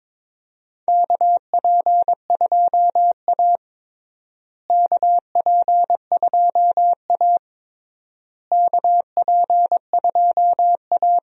Morse code, heard as a keyed tone, about 22 wpm, 700 Hz. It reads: KP2A KP2A KP2A
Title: DXCC-100 Call Signs - 4 Characters - Repeated 3x - 22wpm